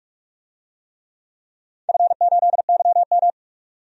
Morse code, heard as a keyed tone, about 47 wpm, 700 Hz. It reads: F8YM
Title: DXCC-100 Call Signs - 4 Characters - 45wpm